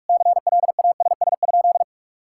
Morse code, heard as a keyed tone, about 46 wpm, 700 Hz. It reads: CLASS?
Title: Sentences from Top 1000 Words - Review Part 6 - 45wpm